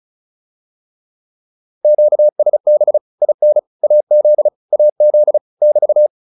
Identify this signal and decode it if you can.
Morse code, heard as a keyed tone, about 35 wpm, 600 Hz. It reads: QSB IN AZ AZ <BT>
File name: QSO204 35